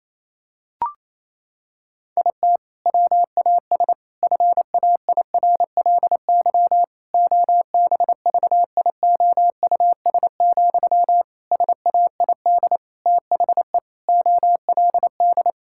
Morse code, heard as a keyed tone, about 28 wpm, 700 Hz. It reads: IT WAS FAIRLY OBVIOUS, SAID THE OLD
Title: Sentences from Top 1600 Words 28wpm